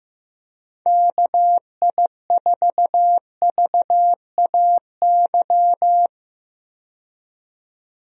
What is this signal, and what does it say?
Morse code, heard as a keyed tone, about 15 wpm, 700 Hz. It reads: KI4VAY
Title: USA Call Signs - 6 Characters - 15wpm